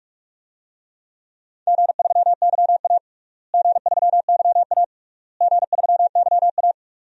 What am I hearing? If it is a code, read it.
Morse code, heard as a keyed tone, about 46 wpm, 700 Hz. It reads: G3YA G3YA G3YA
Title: DXCC-100 Call Signs - 4 Characters - Repeated 3x - 45wpm